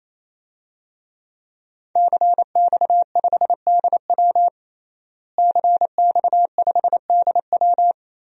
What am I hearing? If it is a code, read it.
Morse code, heard as a keyed tone, about 27 wpm, 700 Hz. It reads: CX5DW CX5DW
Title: DXCC-100 Call Signs - 5 Characters - Repeated 3x - 28wpm